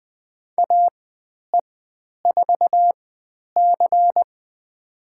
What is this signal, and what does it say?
Morse code, heard as a keyed tone, about 20 wpm, spaced as for 10 wpm, 700 Hz. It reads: AE4C
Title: 109 Lesson 18g - Letter m - Call Signs 20wpm